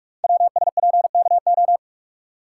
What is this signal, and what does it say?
Morse code, heard as a keyed tone, about 44 wpm, 700 Hz. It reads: WSPKO
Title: Code Groups - 5 Letters - 45wpm